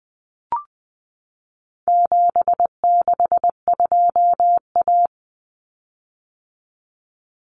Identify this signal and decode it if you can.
Morse code, heard as a keyed tone, about 20 wpm, 700 Hz. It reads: 762A
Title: Common Cut Numbers - 4-Digit 20wpm